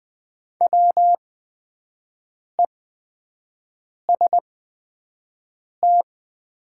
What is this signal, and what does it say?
Morse code, heard as a keyed tone, about 20 wpm, spaced as for 5 wpm, 700 Hz. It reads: WEST
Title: 8x Character Spacing - Top 500 Words - 20wpm